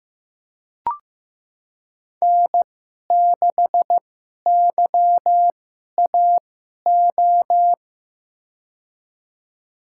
Morse code, heard as a keyed tone, about 15 wpm, spaced as for 11 wpm, 700 Hz. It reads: N6YAO
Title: USA Call Signs - 5 Characters - 2x Character Spacing - 15wpm